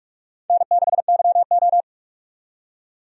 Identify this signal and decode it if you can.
Morse code, heard as a keyed tone, about 48 wpm, 700 Hz. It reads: N6YO
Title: USA Call Signs - 4 Characters - 45wpm